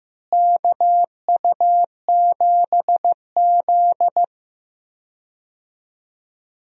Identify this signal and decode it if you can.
Morse code, heard as a keyed tone, about 15 wpm, 700 Hz. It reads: KU7Z
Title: USA Call Signs - 4 Characters - Repeated 3x - 15wpm